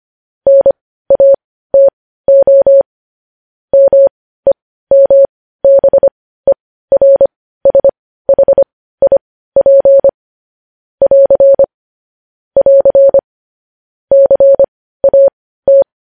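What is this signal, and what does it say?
Morse code, heard as a keyed tone, about 25 wpm, spaced as for 15 wpm, 550 Hz. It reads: NATO MEMBERSHIP <AR> <AR> CAT
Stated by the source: Morse code news headlines at 15 WPM.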